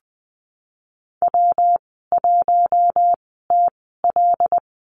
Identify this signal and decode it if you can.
Morse code, heard as a keyed tone, about 20 wpm, spaced as for 15 wpm, 700 Hz. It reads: W1TL
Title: 122 Lesson 20g - Number 3 - Call Signs 20wpm